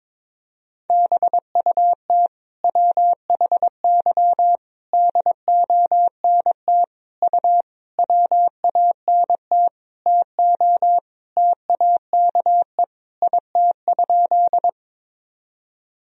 Morse code, heard as a keyed tone, about 22 wpm, 700 Hz. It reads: BUT WHY DONT U WANT TO TAKE IT?
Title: Sentences from Top 300 Words and Abbreviations 22wpm